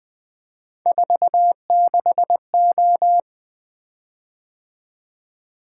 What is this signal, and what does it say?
Morse code, heard as a keyed tone, about 20 wpm, 700 Hz. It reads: 46O
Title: Three Letter-Number Combinations - 20wpm